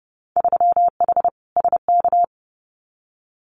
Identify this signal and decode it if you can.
Morse code, heard as a keyed tone, about 32 wpm, 700 Hz. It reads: 3H SK
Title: Winter Field Day Exchanges 30wpm